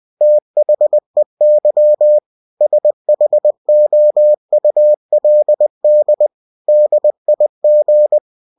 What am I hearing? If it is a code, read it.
Morse code, heard as a keyed tone, about 20 wpm, 600 Hz. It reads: THEY SHOULD DIG
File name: winnie-20wpm